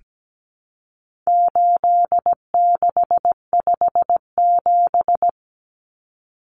Morse code, heard as a keyed tone, about 17 wpm, 700 Hz. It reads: 8657
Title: Common Cut Numbers - 4-Digit 17wpm